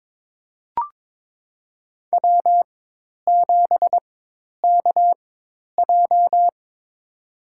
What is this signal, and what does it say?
Morse code, heard as a keyed tone, about 22 wpm, spaced as for 10 wpm, 700 Hz. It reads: W7KJ
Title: USA Call Signs - 4 Characters - 4x Character Spacing - 22wpm